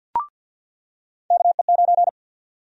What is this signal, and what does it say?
Morse code, heard as a keyed tone, about 53 wpm, 700 Hz. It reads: KE9
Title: USA Call Sign Prefixes - 50wpm